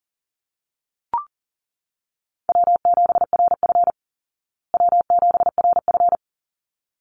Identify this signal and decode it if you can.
Morse code, heard as a keyed tone, about 40 wpm, 700 Hz. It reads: W7RF W7RF
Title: Easy Call Signs Repeated 2x - 40wpm